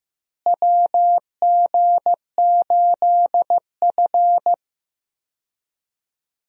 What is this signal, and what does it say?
Morse code, heard as a keyed tone, about 16 wpm, 700 Hz. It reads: WG8F
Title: USA Call Signs - 4 Characters - Repeated 2x - 15wpm